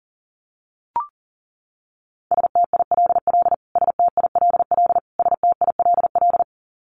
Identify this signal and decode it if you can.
Morse code, heard as a keyed tone, about 40 wpm, 700 Hz. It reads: STILL STILL STILL
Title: Top 500 Words Repeated 40wpm